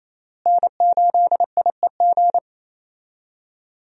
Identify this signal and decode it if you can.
Morse code, heard as a keyed tone, about 28 wpm, 700 Hz. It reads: N8IEG
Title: USA Call Signs - 5 Characters - Repeated 2x - 28wpm